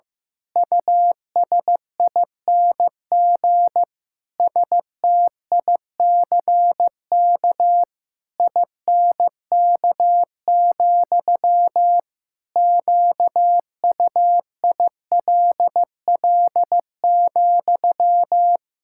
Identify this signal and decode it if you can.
Morse code, heard as a keyed tone, about 15 wpm, 700 Hz. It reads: USING STICK INK, QUILL,
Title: Pangram Challenge v2 - 15wpm